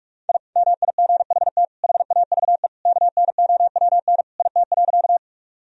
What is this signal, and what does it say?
Morse code, heard as a keyed tone, about 46 wpm, 700 Hz. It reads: I MIGHT HAVE KNOWN IT.
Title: Sentences from Top 1000 Words - Review Part 3 - 45wpm